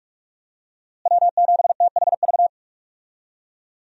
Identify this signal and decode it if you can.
Morse code, heard as a keyed tone, about 45 wpm, 700 Hz. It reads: W7THV
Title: USA Call Signs - 5 Characters - Repeated 2x - 45wpm